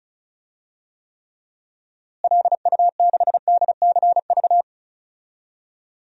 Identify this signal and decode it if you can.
Morse code, heard as a keyed tone, about 36 wpm, 700 Hz. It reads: LU6DCV
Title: DXCC-100 Call Signs - 6 Characters - Repeated 2x - 35wpm